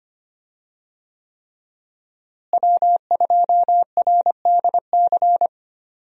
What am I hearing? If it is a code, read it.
Morse code, heard as a keyed tone, about 25 wpm, 700 Hz. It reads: W2RDC
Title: USA Call Signs - 5 Characters - Repeated 3x - 25wpm